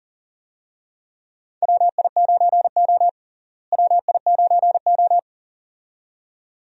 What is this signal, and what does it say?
Morse code, heard as a keyed tone, about 40 wpm, 700 Hz. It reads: WI9O WI9O
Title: USA Call Signs - 4 Characters - Repeated 2x - 40wpm